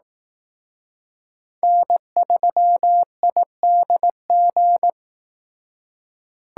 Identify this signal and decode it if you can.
Morse code, heard as a keyed tone, about 18 wpm, 700 Hz. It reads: N3IDG
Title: Speed Racing - USA 1x3 Call Signs 15wpm